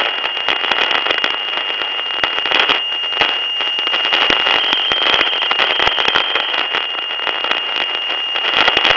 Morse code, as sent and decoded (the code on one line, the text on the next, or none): none